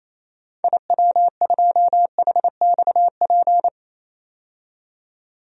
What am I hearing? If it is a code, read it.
Morse code, heard as a keyed tone, about 28 wpm, 700 Hz. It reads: IW2HXP
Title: DXCC-100 Call Signs - 6 Characters - 28wpm